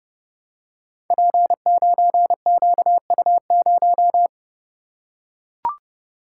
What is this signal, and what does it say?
Morse code, heard as a keyed tone, about 29 wpm, 700 Hz. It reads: P9QU0
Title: Code Groups - 5 Letter:Numbers - 30wpm